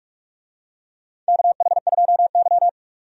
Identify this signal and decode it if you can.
Morse code, heard as a keyed tone, about 46 wpm, 700 Hz. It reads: KH2Y
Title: USA Call Signs - 4 Characters - 45wpm